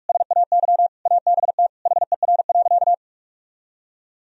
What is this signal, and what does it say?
Morse code, heard as a keyed tone, about 46 wpm, 700 Hz. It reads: SAY ABT HER.
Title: Sentences from Top 200 Words and Abbreviations 45wpm